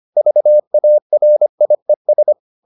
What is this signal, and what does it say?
Morse code, heard as a keyed tone, about 25 wpm, 600 Hz. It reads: VARIES